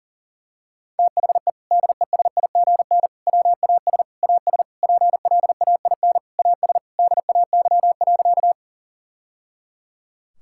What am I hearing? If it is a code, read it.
Morse code, heard as a keyed tone, about 40 wpm, 700 Hz. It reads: THE DESIGN WAS AS PLAIN AS DAY.
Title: Sentences from Top 1000 Words - Review Part 5 - 40wpm